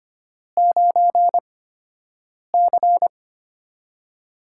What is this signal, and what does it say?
Morse code, heard as a keyed tone, about 25 wpm, spaced as for 7 wpm, 700 Hz. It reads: 9C
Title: USA Call Signs - 8x Character Spacing - 25wpm